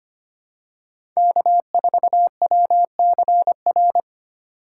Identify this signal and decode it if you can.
Morse code, heard as a keyed tone, about 25 wpm, 700 Hz. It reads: K4WCR
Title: USA Call Signs - 5 Characters - 3x Character Spacing - 25wpm